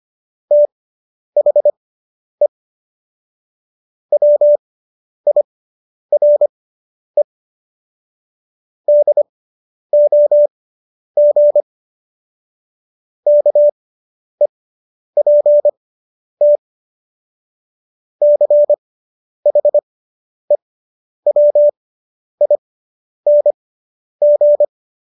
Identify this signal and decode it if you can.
Morse code, heard as a keyed tone, about 25 wpm, spaced as for 10 wpm, 590 Hz. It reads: THE WIRE DOG KEPT CHEWING